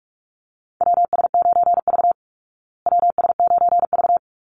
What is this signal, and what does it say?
Morse code, heard as a keyed tone, about 44 wpm, 700 Hz. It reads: WS9V WS9V
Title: Call Signs Repeated 3x - 45wpm